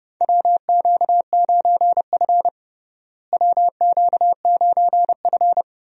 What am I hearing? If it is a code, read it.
Morse code, heard as a keyed tone, about 32 wpm, 700 Hz. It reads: WQ9F WQ9F
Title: USA Call Signs - 4 Characters - Repeated 2x - 30wpm